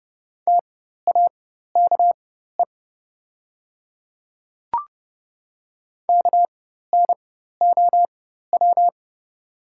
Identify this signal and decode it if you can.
Morse code, heard as a keyed tone, about 30 wpm, spaced as for 14 wpm, 700 Hz. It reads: TAKE KNOW
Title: 4x Character Spacing - Top 300 Words - 30wpm